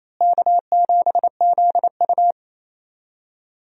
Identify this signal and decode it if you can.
Morse code, heard as a keyed tone, about 28 wpm, 700 Hz. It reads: K7ZU
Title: USA Call Signs - 4 Characters - Repeated 2x - 28wpm